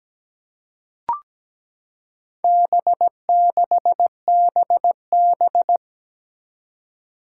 Morse code, heard as a keyed tone, about 17 wpm, 700 Hz. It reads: B6BB
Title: Four Challenging Letter-Number Combinations 17wpm